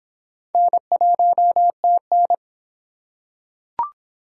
Morse code, heard as a keyed tone, about 26 wpm, 700 Hz. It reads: N1TN